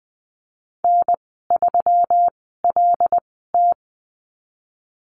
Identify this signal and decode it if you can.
Morse code, heard as a keyed tone, about 19 wpm, spaced as for 14 wpm, 700 Hz. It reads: N3LT
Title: 167 Lesson 27g - Number 7 - Call Signs 20wpm